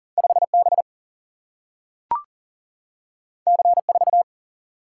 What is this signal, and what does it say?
Morse code, heard as a keyed tone, about 40 wpm, 700 Hz. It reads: 5 B C 4
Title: Single-Letter Single-Number 40wpm